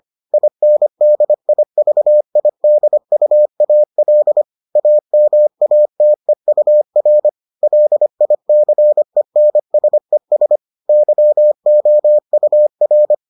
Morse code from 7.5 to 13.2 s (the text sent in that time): LICENSES YOUR